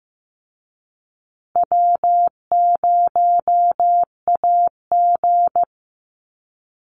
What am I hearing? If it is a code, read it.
Morse code, heard as a keyed tone, about 15 wpm, 700 Hz. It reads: W0AG
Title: Call Signs Repeated 2x - 15wpm